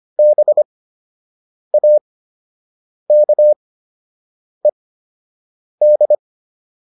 Morse code, heard as a keyed tone, about 25 wpm, spaced as for 7 wpm, 600 Hz. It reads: BAKED